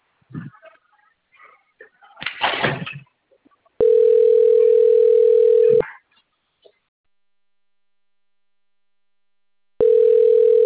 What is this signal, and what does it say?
Ringback tone (North America), 440 + 480 Hz.
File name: rg-1600-unknown- - - .6693